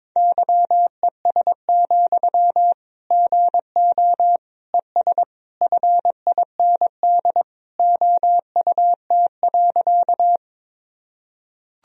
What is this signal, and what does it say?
Morse code, heard as a keyed tone, about 22 wpm, 700 Hz. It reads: YES, GO ES FIND OUT.
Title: Sentences from Top 500 Words and Abbreviations 22wpm